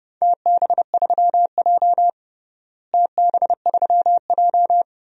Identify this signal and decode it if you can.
Morse code, heard as a keyed tone, about 32 wpm, 700 Hz. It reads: TB3J TB3J
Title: DXCC-100 Call Signs - 4 Characters - Repeated 2x - 30wpm